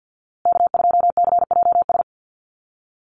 Morse code, heard as a keyed tone, about 48 wpm, 700 Hz. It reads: K2CJS
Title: USA Call Signs Repeated 3x 50wpm